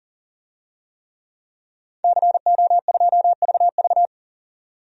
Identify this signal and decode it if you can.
Morse code, heard as a keyed tone, about 39 wpm, 700 Hz. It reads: CO2VV
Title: DXCC-100 Call Signs - 5 Characters - 40wpm